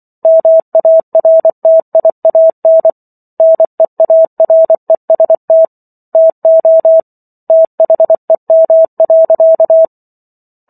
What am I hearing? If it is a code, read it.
Morse code, heard as a keyed tone, about 24 wpm, 650 Hz. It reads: MARTIAN NEAREST TO THEM.